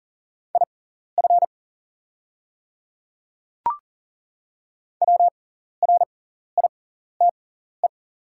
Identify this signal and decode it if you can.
Morse code, heard as a keyed tone, about 40 wpm, spaced as for 14 wpm, 700 Hz. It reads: IF WRITE
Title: 6x Character Spacing - Top 200 Words - 40wpm